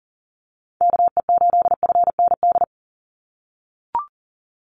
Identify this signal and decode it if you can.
Morse code, heard as a keyed tone, about 40 wpm, 700 Hz. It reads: KE8FND
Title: USA Call Signs 40wpm